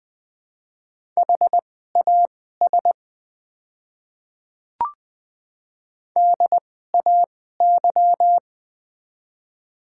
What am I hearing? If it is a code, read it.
Morse code, heard as a keyed tone, about 20 wpm, spaced as for 14 wpm, 700 Hz. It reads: HAS DAY
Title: 2x Character Spacing - Top 100 Words - 20wpm